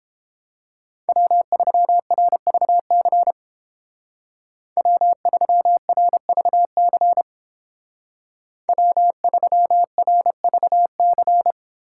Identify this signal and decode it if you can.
Morse code, heard as a keyed tone, about 30 wpm, 700 Hz. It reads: W3RVC W3RVC W3RVC